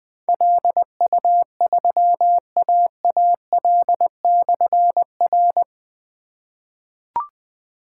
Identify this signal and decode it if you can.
Morse code, heard as a keyed tone, about 20 wpm, 700 Hz. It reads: LU3AAL/R
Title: Difficult Call Signs 20wpm